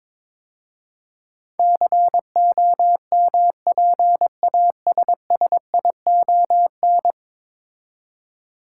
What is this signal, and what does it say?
Morse code, heard as a keyed tone, about 22 wpm, 700 Hz. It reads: COMPASSION
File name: practice / 22wpm / 10 Letter Words - First 3 Letters Given - 22wpm